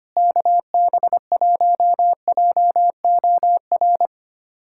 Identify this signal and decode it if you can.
Morse code, heard as a keyed tone, about 25 wpm, 700 Hz. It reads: KB1JOR
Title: USA Callsign DE Callsign - 6 Characters - 25wpm